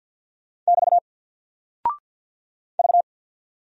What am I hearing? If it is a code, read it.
Morse code, heard as a keyed tone, about 48 wpm, 700 Hz. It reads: <BT> V
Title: Single Character (Letter:Number:Punctuation:Prosign) Alt 50wpm